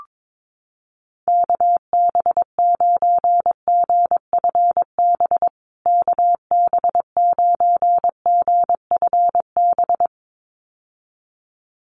Morse code, heard as a keyed tone, about 22 wpm, 700 Hz. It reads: KB9GFB KB9GFB
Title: USA Call Signs Repeated 2x 22wpm